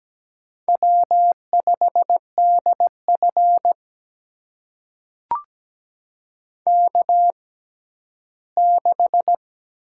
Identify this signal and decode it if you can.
Morse code, heard as a keyed tone, about 17 wpm, spaced as for 12 wpm, 700 Hz. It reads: W5DF K6
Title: USA Call Signs - 4 Characters - 6x Character Spacing - 17wpm